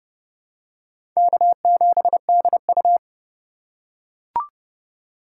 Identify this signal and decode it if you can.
Morse code, heard as a keyed tone, about 32 wpm, 700 Hz. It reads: K7DU
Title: USA Call Signs - 4x Character Spacing - 30wpm